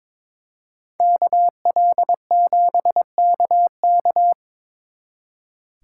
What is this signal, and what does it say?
Morse code, heard as a keyed tone, about 25 wpm, 700 Hz. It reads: KL7KK
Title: Speed Racing - USA Call Signs 22wpm